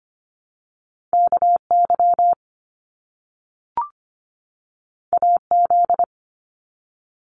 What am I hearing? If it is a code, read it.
Morse code, heard as a keyed tone, about 25 wpm, 700 Hz. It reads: KY AZ
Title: US States 25wpm